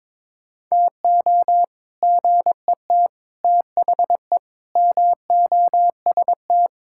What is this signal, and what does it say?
Morse code, heard as a keyed tone, about 22 wpm, 700 Hz. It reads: TO GET THE MOST
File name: practice / 22wpm / Sets of 6 Words - 22wpm